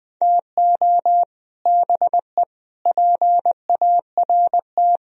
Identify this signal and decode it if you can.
Morse code, heard as a keyed tone, about 20 wpm, 700 Hz. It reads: TO BE PART
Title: Speed-Racing - Sets of 4 Words 20wpm